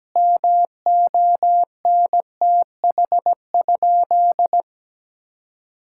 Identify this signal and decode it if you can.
Morse code, heard as a keyed tone, about 17 wpm, 700 Hz. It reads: MONTH?
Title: Sentences from Top 1000 Words - Review Part 5 - 17wpm